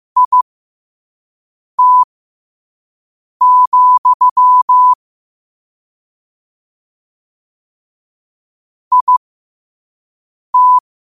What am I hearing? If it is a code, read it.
Morse code, heard as a keyed tone, about 15 wpm, spaced as for 5 wpm, 1000 Hz. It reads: IT, IT
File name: qotdfiles / 5wpm